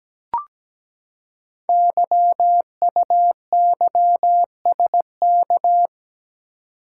Morse code, heard as a keyed tone, about 17 wpm, 700 Hz. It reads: YUYSK
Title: Code Groups - 5 Letters - 17wpm